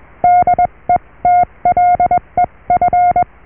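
Morse code, heard as a keyed tone, about 21 wpm, spaced as for 18 wpm, 710 Hz. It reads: DETLEF
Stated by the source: Morse QSO on 14 MHz with a German station, received at RST 579.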